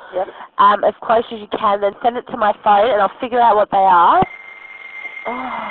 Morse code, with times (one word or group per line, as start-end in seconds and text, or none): none